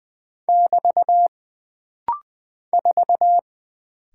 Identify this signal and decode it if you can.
Morse code, heard as a keyed tone, about 20 wpm, 700 Hz. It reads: <BT> 4
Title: Single Character (Letter:Number:Punctuation:Prosign) - Rapid-Fire - 20wpm